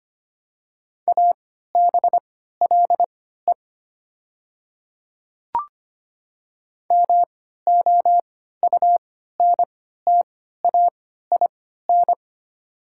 Morse code, heard as a keyed tone, about 25 wpm, spaced as for 14 wpm, 700 Hz. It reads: ABLE MOUNTAIN
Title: 3x Character Spacing - Top 500 Words - 25wpm